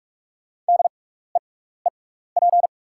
Morse code, heard as a keyed tone, about 44 wpm, spaced as for 15 wpm, 690 Hz. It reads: DEEP
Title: 6x Character Spacing - Top 500 Words - 45wpm